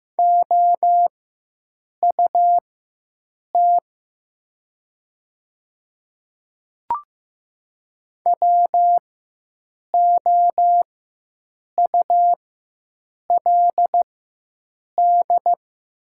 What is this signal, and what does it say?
Morse code, heard as a keyed tone, about 15 wpm, spaced as for 7 wpm, 700 Hz. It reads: OUT WOULD
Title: 4x Character Spacing - Top 100 Words - 15wpm